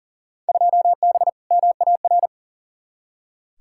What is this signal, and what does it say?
Morse code, heard as a keyed tone, about 40 wpm, 700 Hz. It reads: 2B MAR
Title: Field Day Exchanges Encore - 40wpm